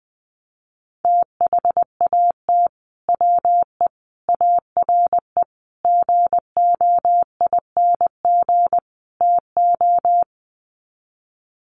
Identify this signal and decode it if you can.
Morse code, heard as a keyed tone, about 20 wpm, 700 Hz. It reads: THAT WE ARE GOING TO
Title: Sets of 5 Words Encore 20wpm